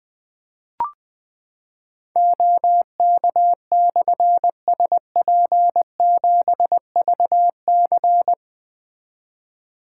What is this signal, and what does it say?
Morse code, heard as a keyed tone, about 20 wpm, 700 Hz. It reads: OK/SP7VC
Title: Difficult Call Signs 20wpm